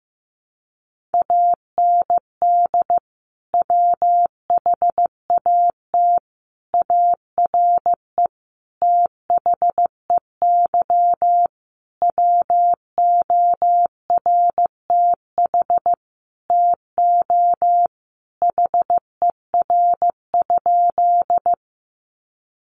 Morse code, heard as a keyed tone, about 15 wpm, 700 Hz. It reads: AND WHAT ARE THEY WORTH TO HER?
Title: Sentences from Top 800 Words 15wpm